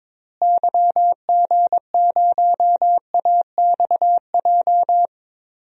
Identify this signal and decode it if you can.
Morse code, heard as a keyed tone, about 22 wpm, 700 Hz. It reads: YG0AXJ
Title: DXCC-100 Call Signs - 6 Characters - Repeated 3x - 22wpm